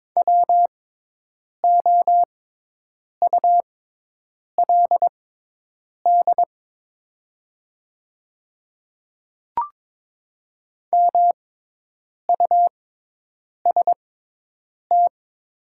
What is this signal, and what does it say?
Morse code, heard as a keyed tone, about 22 wpm, spaced as for 8 wpm, 700 Hz. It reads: WOULD MUST
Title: 6x Character Spacing - Top 300 Words - 22wpm